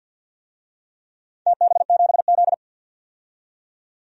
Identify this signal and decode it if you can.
Morse code, heard as a keyed tone, about 46 wpm, 700 Hz. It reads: TB7Z
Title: DXCC-100 Call Signs - 4 Characters - 50wpm